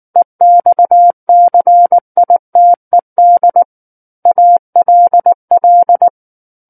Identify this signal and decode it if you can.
Morse code, heard as a keyed tone, about 19 wpm, 700 Hz. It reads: EXCITED ALL
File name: Book One / 10-Chapter 10 - 19 WPM